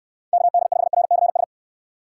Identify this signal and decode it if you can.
Morse code, heard as a keyed tone, about 69 wpm, 700 Hz. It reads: KD5RCS